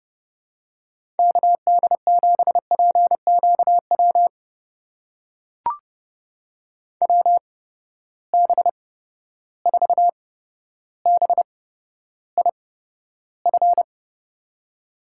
Morse code, heard as a keyed tone, about 30 wpm, spaced as for 13 wpm, 700 Hz. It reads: KD7PQW WB4BIF
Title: USA Call Signs - 6 Characters - 8x Character Spacing - 30wpm